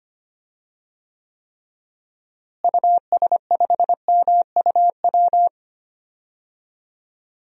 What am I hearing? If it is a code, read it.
Morse code, heard as a keyed tone, about 25 wpm, 700 Hz. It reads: US5MUW
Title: DXCC-100 Call Signs - 6 Characters - Repeated 3x - 25wpm